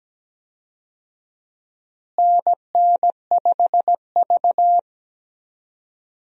Morse code, heard as a keyed tone, about 17 wpm, 700 Hz. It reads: NN5V
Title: USA Call Signs - 4 Characters - Repeated 3x - 17wpm